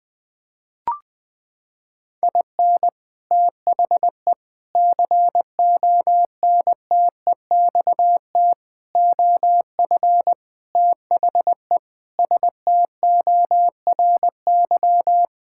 Morse code, heard as a keyed tone, about 21 wpm, 700 Hz. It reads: IN THE CONTEXT OF THE STORY
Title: Sentences with 5-Word Phrases - 20wpm